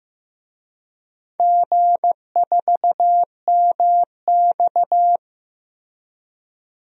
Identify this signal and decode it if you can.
Morse code, heard as a keyed tone, about 15 wpm, 700 Hz. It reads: G4MX
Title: DXCC-100 Call Signs - 4 Characters - Repeated 3x - 15wpm